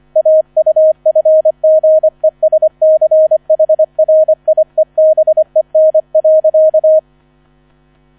Morse code, heard as a keyed tone, about 25 wpm, 630 Hz. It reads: AUFGESCHRIEBEN.